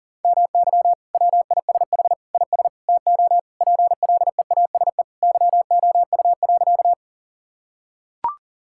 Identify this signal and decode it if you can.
Morse code, heard as a keyed tone, about 40 wpm, 700 Hz. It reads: MY WISH IS TO PLEASE YOU.
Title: Sentences from Top 1600 Words 40wpm